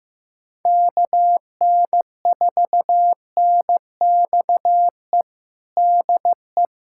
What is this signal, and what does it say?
Morse code, heard as a keyed tone, about 16 wpm, 700 Hz. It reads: KN4NXE DE
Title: USA Callsign DE Callsign 15wpm